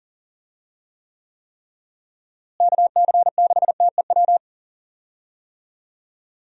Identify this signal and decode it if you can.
Morse code, heard as a keyed tone, about 37 wpm, 700 Hz. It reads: KC6TEW
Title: USA Call Signs - 6 Characters - Repeated 3x - 40wpm